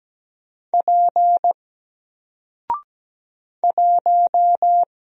Morse code, heard as a keyed tone, about 17 wpm, 700 Hz. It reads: P 1
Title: Single Character (Letter:Number:Punctuation:Prosign) Alt 17wpm